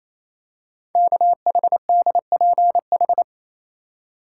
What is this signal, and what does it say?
Morse code, heard as a keyed tone, about 28 wpm, 700 Hz. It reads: KHDPH
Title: Code Groups - 5 Letters - 28wpm